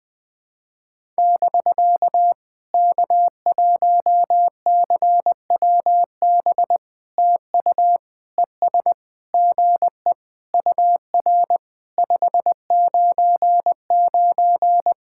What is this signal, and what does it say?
Morse code, heard as a keyed tone, about 20 wpm, 700 Hz. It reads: <BK> K1CWB TU ES GE UR 599
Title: POTA QSO Practice - 20wpm